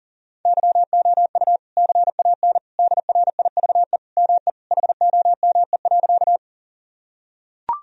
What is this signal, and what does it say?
Morse code, heard as a keyed tone, about 40 wpm, 700 Hz. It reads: YOU CAN DRIVE ME HOME.
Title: Sentences from Top 1000 Words - Review Part 2 - 40wpm